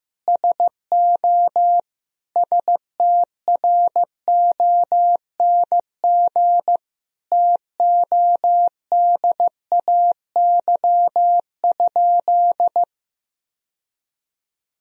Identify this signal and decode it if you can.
Morse code, heard as a keyed tone, about 15 wpm, 700 Hz. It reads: SO STRONG TODAY?
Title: Sentences from Top 1000 Words - Review Part 2 - 15wpm